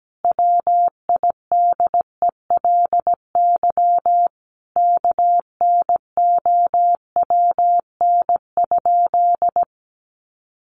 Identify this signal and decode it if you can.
Morse code, heard as a keyed tone, about 17 wpm, 700 Hz. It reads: WIDELY KNOWN?
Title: Sentences from Top 1200 Words 17wpm